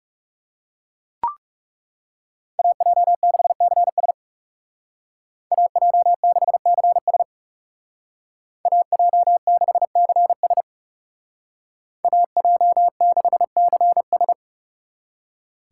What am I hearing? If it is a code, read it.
Morse code, heard as a keyed tone, about 34 wpm, 700 Hz. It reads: AJ6CS AJ6CS AJ6CS AJ6CS
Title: Speed Racing - USA 2x2 Call Signs 30wpm